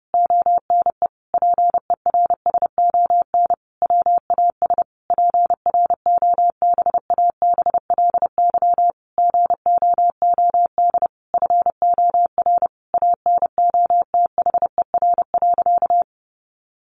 Morse code, heard as a keyed tone, about 30 wpm, 700 Hz. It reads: ONE PERSON WAS PROBABLY GOOD FOR ANOTHER.
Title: Sentences from Top 1400 Words 30wpm